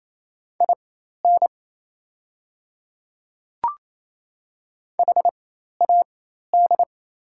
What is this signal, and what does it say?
Morse code, heard as a keyed tone, about 28 wpm, spaced as for 13 wpm, 700 Hz. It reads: IN HAD
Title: 4x Character Spacing - Top 300 Words - 28wpm